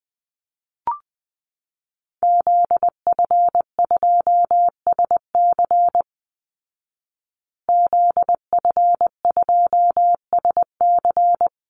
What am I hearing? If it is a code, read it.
Morse code, heard as a keyed tone, about 20 wpm, 700 Hz. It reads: ZF2SC ZF2SC
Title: Call Signs Repeated 3x - 20wpm